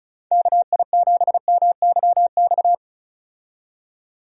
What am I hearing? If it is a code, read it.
Morse code, heard as a keyed tone, about 34 wpm, 700 Hz. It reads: KI7MYX
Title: USA Call Signs - 6 Characters - 35wpm